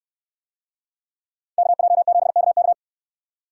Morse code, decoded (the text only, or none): KJ7KK